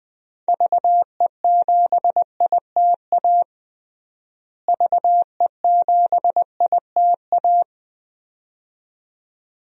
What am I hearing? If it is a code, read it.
Morse code, heard as a keyed tone, about 21 wpm, 700 Hz. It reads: VE7ITA VE7ITA
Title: DXCC-100 Call Signs - 6 Characters - Repeated 2x - 20wpm